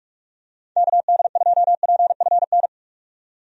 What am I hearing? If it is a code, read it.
Morse code, heard as a keyed tone, about 46 wpm, 700 Hz. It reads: KD2PFN
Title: USA Call Signs - 6 Characters - 45wpm